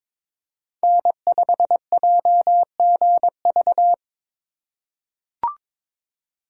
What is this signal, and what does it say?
Morse code, heard as a keyed tone, about 22 wpm, 700 Hz. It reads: N5JGV
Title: USA Call Signs - 3x Character Spacing - 22wpm